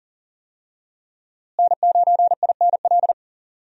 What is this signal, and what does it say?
Morse code, heard as a keyed tone, about 37 wpm, 700 Hz. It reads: N9INL
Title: USA Call Signs - 5 Characters - Repeated 3x - 40wpm